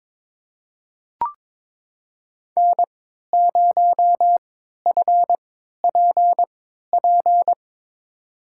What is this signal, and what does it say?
Morse code, heard as a keyed tone, about 22 wpm, spaced as for 13 wpm, 700 Hz. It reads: N0FPP
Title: USA Call Signs - 5 Characters - 3x Character Spacing - 22wpm